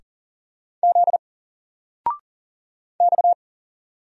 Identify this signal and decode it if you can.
Morse code, heard as a keyed tone, about 40 wpm, 700 Hz. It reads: Z X
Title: Single Letter-Number 40wpm